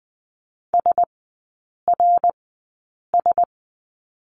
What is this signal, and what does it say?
Morse code, heard as a keyed tone, about 20 wpm, spaced as for 8 wpm, 690 Hz. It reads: SRS
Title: 159 Lesson 26e - Letter g - Three Characters 20wpm